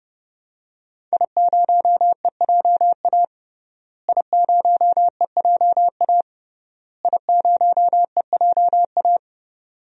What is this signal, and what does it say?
Morse code, heard as a keyed tone, about 30 wpm, 700 Hz. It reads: I0EJA I0EJA I0EJA
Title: DXCC-100 Call Signs - 5 Characters - Repeated 3x - 30wpm